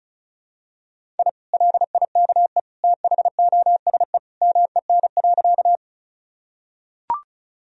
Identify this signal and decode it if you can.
Morse code, heard as a keyed tone, about 34 wpm, 700 Hz. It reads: I LIKE THOSE MEN.
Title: Sentences from Top 1000 Words - Review Part 1 - 35wpm